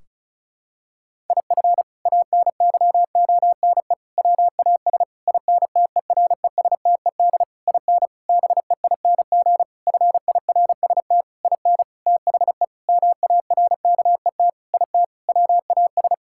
Morse code, heard as a keyed tone, about 34 wpm, 700 Hz. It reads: IF ANYONE WAS INTERESTED IN BEING FIRST IN THE MARKET IT WAS
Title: Sentences from Top 1600 Words 35wpm